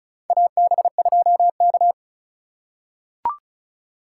Morse code, heard as a keyed tone, about 34 wpm, 700 Hz. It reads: AB2K
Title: USA Call Signs - 4 Characters - 2x Character Spacing - 35wpm